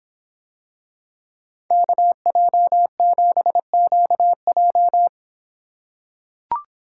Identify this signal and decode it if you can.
Morse code, heard as a keyed tone, about 25 wpm, 700 Hz. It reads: KJ7QJ